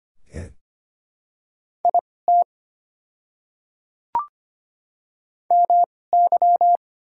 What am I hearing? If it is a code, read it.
Morse code, heard as a keyed tone, about 25 wpm, spaced as for 18 wpm, 700 Hz. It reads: IT MY
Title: 2x Character Spacing - Top 100 Words - 25wpm